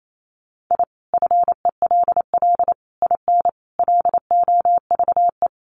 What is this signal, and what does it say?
Morse code, heard as a keyed tone, about 28 wpm, 700 Hz. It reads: I FELL IN LOVE
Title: Sets of 5 Words Encore 28wpm